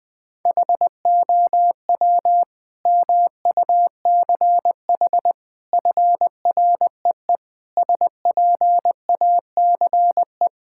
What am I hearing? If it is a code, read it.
Morse code, heard as a keyed tone, about 20 wpm, 700 Hz. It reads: HOW MUCH FREE SPACE
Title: Sentences from Top 1000 Words - Review Part 6 - 20wpm